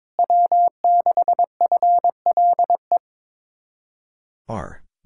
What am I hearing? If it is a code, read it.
Morse code, heard as a keyed tone, about 22 wpm, 700 Hz. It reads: W6FLE
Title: Speed-Racing Common QSO Phrases - 22wpm